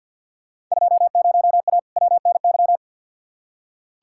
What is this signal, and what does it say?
Morse code, heard as a keyed tone, about 46 wpm, 700 Hz. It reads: 20A WNY
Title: Field Day Exchanges Encore - 50wpm